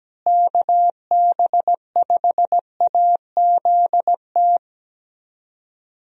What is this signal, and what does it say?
Morse code, heard as a keyed tone, about 17 wpm, 700 Hz. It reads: KB5AZT
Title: USA Call Signs - 6 Characters - Repeated 3x - 17wpm